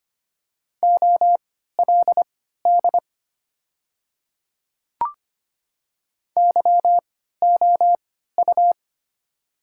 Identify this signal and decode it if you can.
Morse code, heard as a keyed tone, about 25 wpm, spaced as for 14 wpm, 690 Hz. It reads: OLD YOU
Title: 3x Character Spacing - Top 500 Words - 25wpm